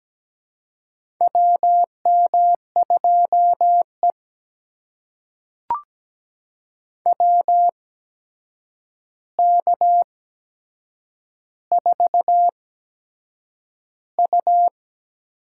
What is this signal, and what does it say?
Morse code, heard as a keyed tone, about 17 wpm, spaced as for 7 wpm, 700 Hz. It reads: WM2E WK4U
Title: USA Call Signs - 4 Characters - 8x Character Spacing - 17wpm